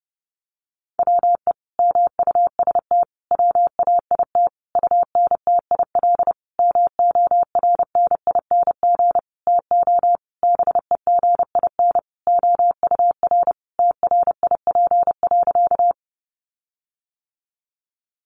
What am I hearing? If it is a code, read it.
Morse code, heard as a keyed tone, about 30 wpm, 700 Hz. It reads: WE MUST WAIT UNTIL MORNING TO BEGIN OUR TRIP.
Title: Sentences from Top 1300 Words 30wpm